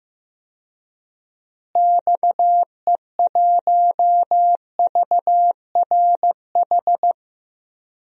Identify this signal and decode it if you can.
Morse code, heard as a keyed tone, about 16 wpm, 700 Hz. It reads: XE1VRH
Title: DXCC-100 Call Signs - 6 Characters - Repeated 3x - 15wpm